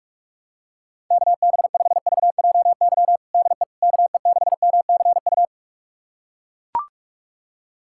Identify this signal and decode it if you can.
Morse code, heard as a keyed tone, about 41 wpm, 700 Hz. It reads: KB5VJY DE KE6MCU
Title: USA Callsign DE Callsign 45wpm